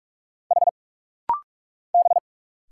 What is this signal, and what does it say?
Morse code, heard as a keyed tone, about 45 wpm, 700 Hz. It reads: H B
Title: Single Letter-Number Alt - Rapid-Fire 45wpm